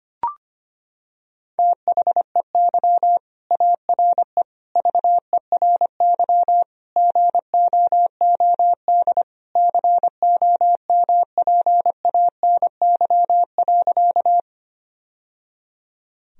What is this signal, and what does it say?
Morse code, heard as a keyed tone, about 25 wpm, 700 Hz. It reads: THEY ARE VERY GOOD COMPANY.
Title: Sentences from Top 1000 Words - Review Part 3 - 25wpm